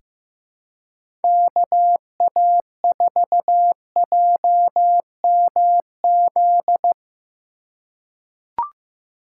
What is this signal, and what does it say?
Morse code, heard as a keyed tone, about 15 wpm, 700 Hz. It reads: KA4JMZ
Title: USA Call Signs - 6 Characters - 2x Character Spacing - 15wpm